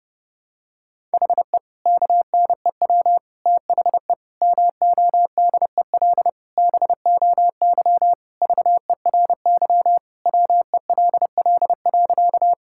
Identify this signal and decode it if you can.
Morse code, heard as a keyed tone, about 30 wpm, 700 Hz. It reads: HE KNEW THE MODEL BOY VERY WELL.
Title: Sentences from Top 1000 Words - Review Part 5 - 30wpm